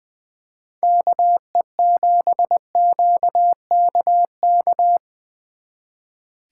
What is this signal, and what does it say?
Morse code, heard as a keyed tone, about 24 wpm, 700 Hz. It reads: KE7QKK
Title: Speed Racing - USA 2x3 Call Signs 20wpm